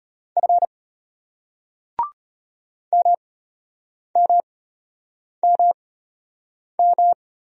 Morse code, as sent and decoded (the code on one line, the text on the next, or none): ..-. / -- -- -- --
F MMMM